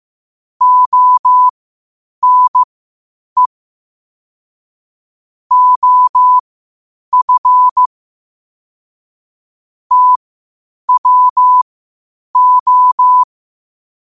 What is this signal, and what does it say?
Morse code, heard as a keyed tone, about 15 wpm, spaced as for 8 wpm, 1000 Hz. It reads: ONE OF TWO
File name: qotdfiles / 8wpm